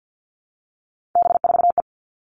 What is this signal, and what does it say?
Morse code, heard as a keyed tone, about 50 wpm, 700 Hz. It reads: B4E
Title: All Cut Numbers - 3-Digit 50wpm